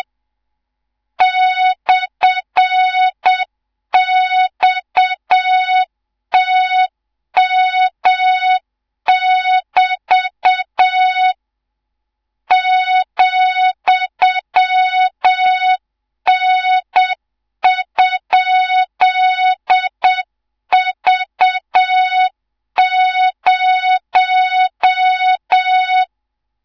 Morse code, heard as a keyed tone, about 7 wpm, 3700 Hz, with noise. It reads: /XTM<BT> ,N?V0
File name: Morze-HA2MN / hang / 18 IRASJELEK2